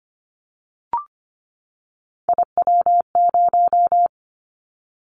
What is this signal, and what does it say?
Morse code, heard as a keyed tone, about 25 wpm, 690 Hz. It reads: IW0
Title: Callsign Prefixes for Top 50 DXCC - 25wpm